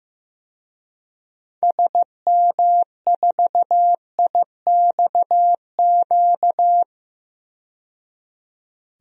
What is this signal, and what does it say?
Morse code, heard as a keyed tone, about 16 wpm, 700 Hz. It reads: SM4IXQ
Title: DXCC-100 Call Signs - 6 Characters - 15wpm